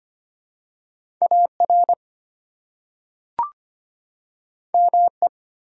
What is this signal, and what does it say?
Morse code, heard as a keyed tone, about 25 wpm, 700 Hz. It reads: AR ME
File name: practice / 25wpm / ARRL Sections v2 - 25wpm